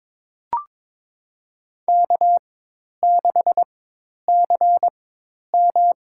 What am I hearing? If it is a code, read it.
Morse code, heard as a keyed tone, about 22 wpm, spaced as for 10 wpm, 700 Hz. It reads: K6CM
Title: USA Call Signs - 4 Characters - 4x Character Spacing - 22wpm